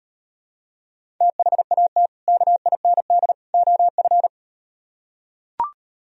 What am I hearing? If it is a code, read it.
Morse code, heard as a keyed tone, about 38 wpm, 700 Hz. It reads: THAT KIND OF